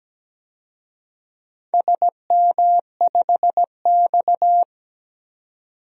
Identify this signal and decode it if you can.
Morse code, heard as a keyed tone, about 17 wpm, 700 Hz. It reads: SM5X
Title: DXCC-100 Call Signs - 4 Characters - 17wpm